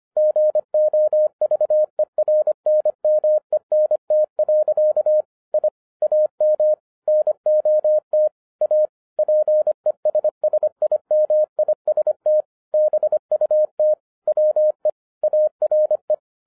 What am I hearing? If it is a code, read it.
Morse code, heard as a keyed tone, about 25 wpm, 610 Hz. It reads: GOVERNMENT. I AM NOT A PESSIMIST BUT WE ARE
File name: 2011-07-mp3 / n39 THE 3.11 2 25wpm